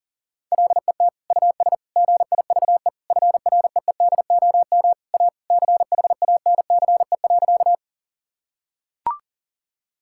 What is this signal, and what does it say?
Morse code, heard as a keyed tone, about 40 wpm, 700 Hz. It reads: LET US GIVE FREEDOM A CHANCE.
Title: Sentences from Top 1000 Words - Review Part 5 - 40wpm